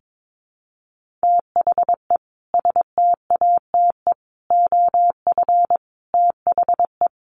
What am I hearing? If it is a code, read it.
Morse code, heard as a keyed tone, about 22 wpm, 700 Hz. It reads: THE STATE OF THE
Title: Sentences from Top 1400 Words 22wpm